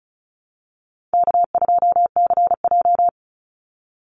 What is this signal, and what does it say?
Morse code, heard as a keyed tone, about 34 wpm, 700 Hz. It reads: K2CJ
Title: Call Signs - 35wpm